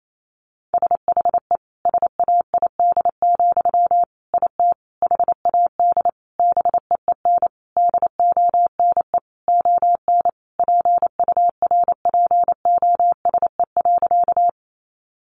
Morse code, heard as a keyed tone, about 28 wpm, 700 Hz. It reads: SHE SAID, IT HAD BEEN DONE ON PURPOSE.
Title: Sentences from Top 800 Words 28wpm